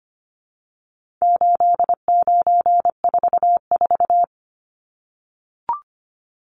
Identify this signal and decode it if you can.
Morse code, heard as a keyed tone, about 25 wpm, 700 Hz. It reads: 8944
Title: Numbers - 4-Digit 25wpm